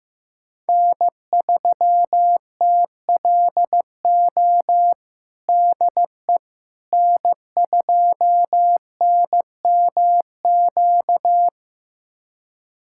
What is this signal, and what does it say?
Morse code, heard as a keyed tone, about 15 wpm, 700 Hz. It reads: N3TLO DE N2NMQ
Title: USA Callsign DE Callsign - 5 Characters - 15wpm